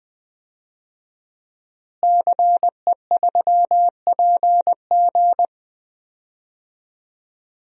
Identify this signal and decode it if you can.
Morse code, heard as a keyed tone, about 20 wpm, 700 Hz. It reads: CE3PG
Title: DXCC-100 Call Signs - 5 Characters - Repeated 2x - 20wpm